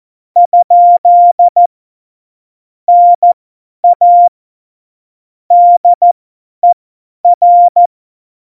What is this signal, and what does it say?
Morse code, heard as a keyed tone, about 14 wpm, spaced as for 10 wpm, 700 Hz. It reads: ? NA DER
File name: Morsekurs / Witze / Witz 01. Mai 2018, 50 F70